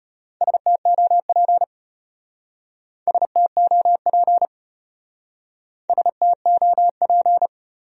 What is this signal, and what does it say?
Morse code, heard as a keyed tone, about 33 wpm, 700 Hz. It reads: STOP STOP STOP